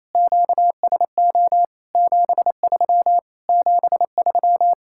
Morse code, heard as a keyed tone, about 34 wpm, 700 Hz. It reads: QSO 73 73
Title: Speed-Racing Common QSO Phrases - 28wpm